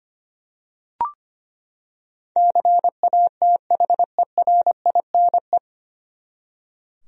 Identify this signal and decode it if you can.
Morse code, heard as a keyed tone, about 25 wpm, 700 Hz. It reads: CATHERINE
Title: USA Common Names - 25wpm